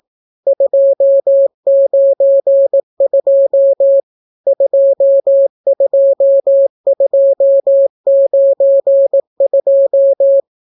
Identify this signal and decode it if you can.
Morse code, heard as a keyed tone, about 18 wpm, 550 Hz. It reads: 292 22292